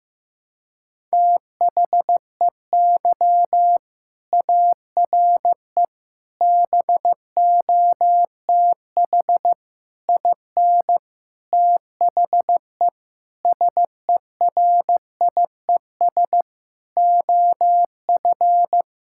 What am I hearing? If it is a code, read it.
Morse code, heard as a keyed tone, about 15 wpm, 700 Hz. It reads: THEY ARE BOTH IN THE SERIES OF
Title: Sentences from Top 1000 Words - Review Part 5 - 15wpm